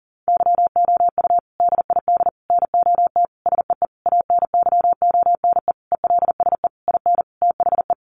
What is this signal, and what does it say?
Morse code, heard as a keyed tone, about 40 wpm, 700 Hz. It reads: YOU DID NOT SEE ANYONE ELSE IN THE
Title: Sentences from Top 900 Words 40wpm